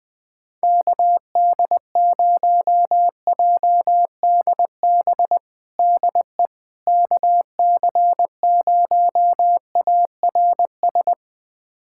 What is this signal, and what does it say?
Morse code, heard as a keyed tone, about 20 wpm, 700 Hz. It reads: KD0JDB DE KC0ARS
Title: USA Callsign DE Callsign - 6 Characters - 20wpm